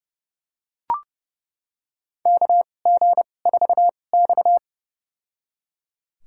Morse code, heard as a keyed tone, about 32 wpm, spaced as for 22 wpm, 700 Hz. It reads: KG4X
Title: USA Call Signs - 2x Character Spacing - 30wpm